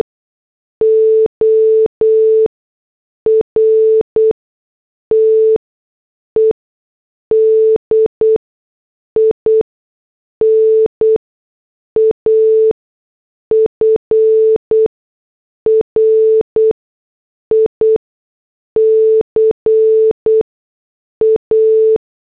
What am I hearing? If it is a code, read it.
Morse code, heard as a keyed tone, about 8 wpm, spaced as for 6 wpm, 440 Hz. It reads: ORTED IN AFRICA